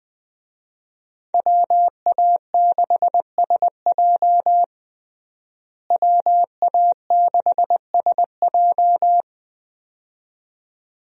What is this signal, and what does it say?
Morse code, heard as a keyed tone, about 20 wpm, 700 Hz. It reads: WA6SJ WA6SJ
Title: USA Call Signs - 5 Characters - Repeated 2x - 20wpm